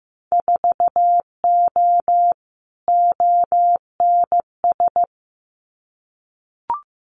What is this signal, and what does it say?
Morse code, heard as a keyed tone, about 15 wpm, 700 Hz. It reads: 4O ONS
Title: Winter Field Day Exchanges 15wpm